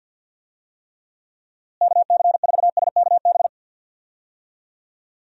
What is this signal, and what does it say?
Morse code, heard as a keyed tone, about 51 wpm, 700 Hz. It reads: KC4SKB
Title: USA Call Signs - 6 Characters - Repeated 2x - 50wpm